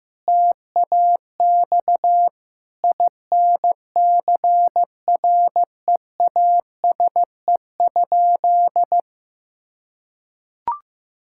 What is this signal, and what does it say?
Morse code, heard as a keyed tone, about 15 wpm, 700 Hz. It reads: TAX INCREASE?
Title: Sentences from Top 1000 Words - Review Part 5 - 15wpm